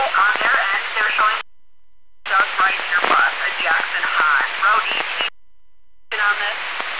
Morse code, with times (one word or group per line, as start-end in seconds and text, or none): none